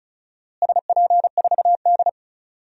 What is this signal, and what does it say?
Morse code, heard as a keyed tone, about 34 wpm, 700 Hz. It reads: SP4D
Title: DXCC-100 Call Signs - 4 Characters - Repeated 3x - 35wpm